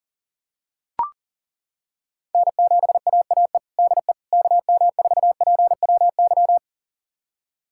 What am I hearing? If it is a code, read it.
Morse code, heard as a keyed tone, about 40 wpm, 700 Hz. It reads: N7AAE DE KM4PWY
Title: USA Callsign DE Callsign 40wpm